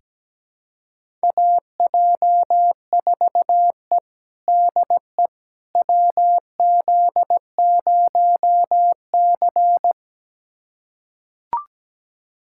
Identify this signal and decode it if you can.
Morse code, heard as a keyed tone, about 17 wpm, 700 Hz. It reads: AJ4E DE WZ0C
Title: USA Callsign DE Callsign - 4 Characters - 17wpm